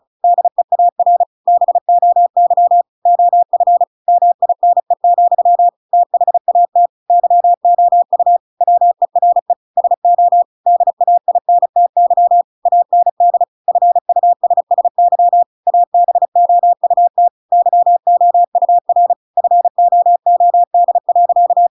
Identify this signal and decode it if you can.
Morse code, heard as a keyed tone, about 35 wpm, 700 Hz. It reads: DEAR BOY OF MINE, THAT YOU WERE SO DAINTY AND FUSSY ABOUT YOUR FOOD.
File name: Pinocchio / Pinocchio ch7 35 35 ik1hge